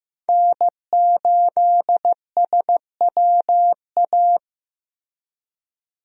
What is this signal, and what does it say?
Morse code, heard as a keyed tone, about 15 wpm, 700 Hz. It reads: N8SWA
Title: USA Call Signs - 5 Characters - Repeated 2x - 15wpm